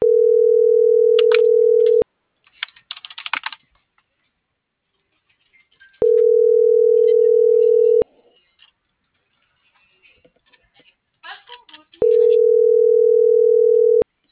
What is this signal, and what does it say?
Ringback tone (North America), 440 + 480 Hz.